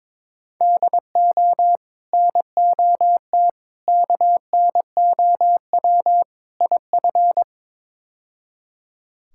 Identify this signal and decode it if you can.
Morse code, heard as a keyed tone, about 22 wpm, 700 Hz. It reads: DO NOT KNOW IF
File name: practice / 22wpm / Build A Phrase - 5-Word Phrases - 22wpm